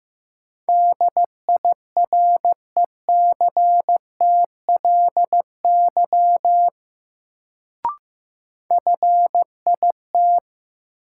Morse code, heard as a keyed tone, about 15 wpm, 700 Hz. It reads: DIRECTLY FIT
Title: Sentences from Top 800 Words - New Word Drill - 15wpm